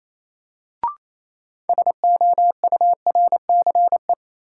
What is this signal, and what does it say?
Morse code, heard as a keyed tone, about 28 wpm, 700 Hz. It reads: SOURCE
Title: Sentences from Top 800 Words - New Word Drill - 28wpm